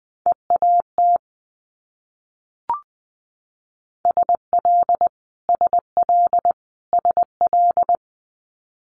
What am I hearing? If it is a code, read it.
Morse code, heard as a keyed tone, about 20 wpm, 700 Hz. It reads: EAT SL SL SL
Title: Multi-letter Phonemes Repeated 3x 20wpm